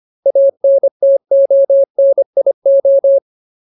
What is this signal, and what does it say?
Morse code, heard as a keyed tone, about 25 wpm, 550 Hz. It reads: ANTONIO
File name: Ultimate Word List / 25WPM Ultimate